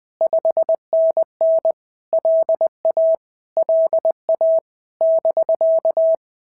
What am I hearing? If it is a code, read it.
Morse code, heard as a keyed tone, about 20 wpm, 650 Hz. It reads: 5NN LA LA <BK>
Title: POTA QSO Practice - 20wpm